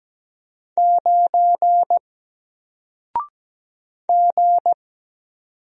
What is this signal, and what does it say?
Morse code, heard as a keyed tone, about 17 wpm, 700 Hz. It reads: 9 G
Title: Single Character (Letter:Number:Punctuation:Prosign) 17wpm